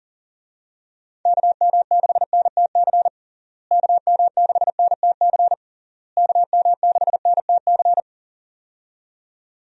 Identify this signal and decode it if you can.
Morse code, heard as a keyed tone, about 40 wpm, 700 Hz. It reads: KM6NTC KM6NTC KM6NTC
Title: USA Call Signs - 6 Characters - Repeated 3x - 40wpm